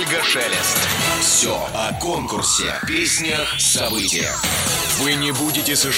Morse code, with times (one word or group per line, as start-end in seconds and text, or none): none